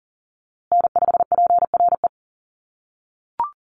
Morse code, heard as a keyed tone, about 40 wpm, 700 Hz. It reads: N5PRE
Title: USA Call Signs Repeated 2x 40wpm